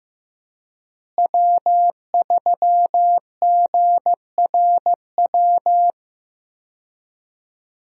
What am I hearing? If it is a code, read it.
Morse code, heard as a keyed tone, about 15 wpm, 700 Hz. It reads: W3GRW
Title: USA Call Signs - 5 Characters - Repeated 2x - 15wpm